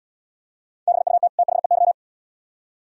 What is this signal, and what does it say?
Morse code, heard as a keyed tone, about 76 wpm, 700 Hz. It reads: BUT THEY